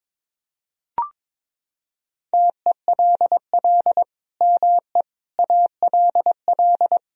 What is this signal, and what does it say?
Morse code, heard as a keyed tone, about 22 wpm, 700 Hz. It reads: TELL ME ALL
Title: Sentences from Top 200 Words and Abbreviations 22wpm